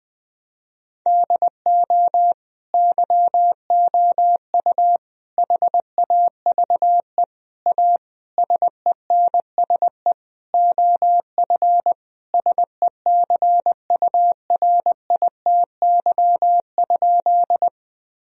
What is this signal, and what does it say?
Morse code, heard as a keyed tone, about 21 wpm, 700 Hz. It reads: DO YOU HAVE A SENSE OF SECURITY?
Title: Sentences from Top 1000 Words - Review Part 4 - 20wpm